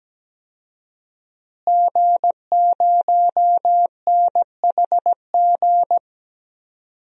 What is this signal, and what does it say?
Morse code, heard as a keyed tone, about 17 wpm, 700 Hz. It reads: G0NHG
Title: DXCC-100 Call Signs - 5 Characters - Repeated 2x - 17wpm